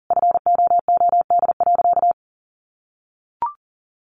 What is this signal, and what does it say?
Morse code, heard as a keyed tone, about 40 wpm, 700 Hz. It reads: FOOD.
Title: Sentences from Top 1100 Words 40wpm